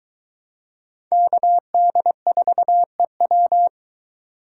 Morse code, heard as a keyed tone, about 21 wpm, 700 Hz. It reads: KD4EW